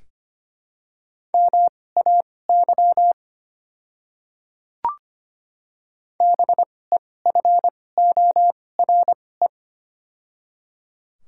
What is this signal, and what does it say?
Morse code, heard as a keyed tone, about 25 wpm, spaced as for 18 wpm, 700 Hz. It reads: MAY BEFORE
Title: 2x Character Spacing - Top 300 Words - 25wpm